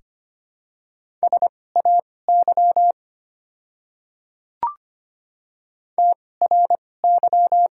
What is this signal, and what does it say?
Morse code, heard as a keyed tone, about 25 wpm, spaced as for 18 wpm, 690 Hz. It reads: SAY TRY
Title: 2x Character Spacing - Top 200 Words - 25wpm